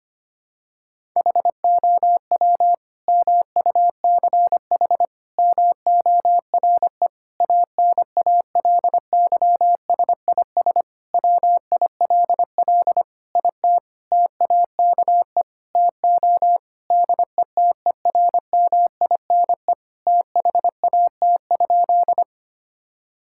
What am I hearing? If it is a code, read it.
Morse code, heard as a keyed tone, about 25 wpm, 700 Hz. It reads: HOW MUCH MORE ANALYSIS WILL IT TAKE TO DETERMINE THAT?
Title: Sentences from Top 1000 Words - Review Part 5 - 25wpm